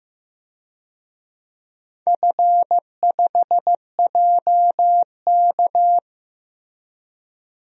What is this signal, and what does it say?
Morse code, heard as a keyed tone, about 15 wpm, 700 Hz. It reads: F5JK
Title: DXCC-100 Call Signs - 4 Characters - Repeated 3x - 15wpm